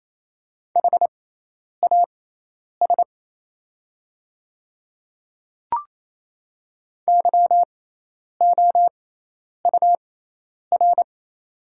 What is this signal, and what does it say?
Morse code, heard as a keyed tone, about 27 wpm, spaced as for 10 wpm, 700 Hz. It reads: HAS YOUR
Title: 6x Character Spacing - Top 100 Words - 28wpm